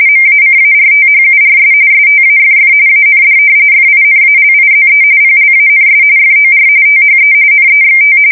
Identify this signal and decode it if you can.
RTTY, about 45.45 baud. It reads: MACARTHURS STAFF AND FINALIZE PLANS FOR THE NEXT
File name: fsk170 / battleshipnj